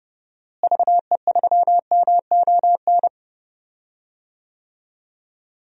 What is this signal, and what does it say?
Morse code, heard as a keyed tone, about 32 wpm, 700 Hz. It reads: VE3MON
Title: DXCC-100 Call Signs - 6 Characters - Repeated 3x - 30wpm